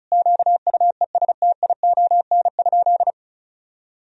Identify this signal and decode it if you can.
Morse code, heard as a keyed tone, about 36 wpm, 700 Hz. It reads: QUESTION?
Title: Sentences from Top 1000 Words - Review Part 4 - 35wpm